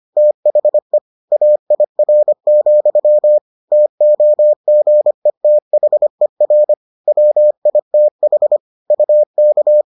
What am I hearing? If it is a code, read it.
Morse code, heard as a keyed tone, about 25 wpm, 600 Hz. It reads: THE AIR, TOGETHER WITH UK